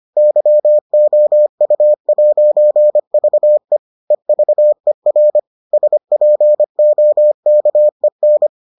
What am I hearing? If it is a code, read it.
Morse code, heard as a keyed tone, about 25 wpm, 600 Hz. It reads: YOU'VE EVER SPOKEN